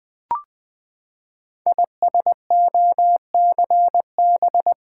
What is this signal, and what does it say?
Morse code, heard as a keyed tone, about 21 wpm, 700 Hz. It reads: ISOCB
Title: Code Groups - 5 Letters - 20wpm